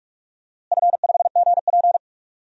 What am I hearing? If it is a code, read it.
Morse code, heard as a keyed tone, about 46 wpm, 700 Hz. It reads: F5GP
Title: DXCC-100 Call Signs - 4 Characters - Repeated 2x - 45wpm